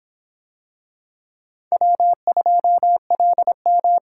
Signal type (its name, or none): Morse code